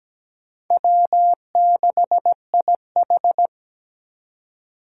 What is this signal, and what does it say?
Morse code, heard as a keyed tone, about 17 wpm, 700 Hz. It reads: W6IH
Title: USA Call Signs - 4 Characters - 2x Character Spacing - 17wpm